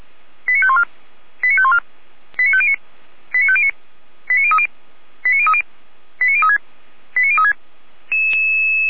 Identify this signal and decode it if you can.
Five-tone selective call (ZVEI), heard as quick runs of five tones, tones of about 70 ms. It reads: 89514 89514 89509 89509 89030 89030 89036 89036
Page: ZVEI